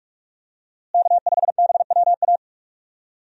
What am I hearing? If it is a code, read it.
Morse code, heard as a keyed tone, about 42 wpm, 700 Hz. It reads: K5BWA
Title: USA Call Signs - 5 Characters - Repeated 3x - 45wpm